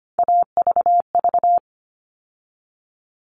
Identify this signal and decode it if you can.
Morse code, heard as a keyed tone, about 25 wpm, 700 Hz. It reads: AVV
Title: All Cut Numbers - 3-Digit 25wpm